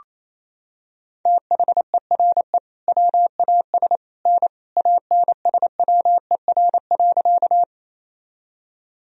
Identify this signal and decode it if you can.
Morse code, heard as a keyed tone, about 27 wpm, 700 Hz. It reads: THERE WAS N ANSWER.
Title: Sentences from Top 300 Words and Abbreviations 28wpm